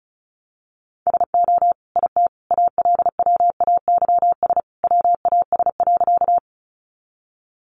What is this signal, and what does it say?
Morse code, heard as a keyed tone, about 34 wpm, 700 Hz. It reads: SO IT ALWAYS WAS.
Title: Sentences from Top 300 Words 35wpm